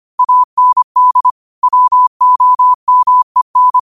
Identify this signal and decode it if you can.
Morse code, heard as a keyed tone, about 25 wpm, 1000 Hz. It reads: AND WOMEN
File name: qotdfiles / 25wpm